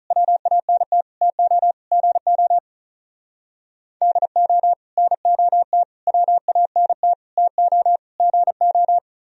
Morse code, heard as a keyed tone, about 43 wpm, 700 Hz. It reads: WANT TO GO DO NOT WANT TO GO
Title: Speed-Racing - Sets of 5 Words 35wpm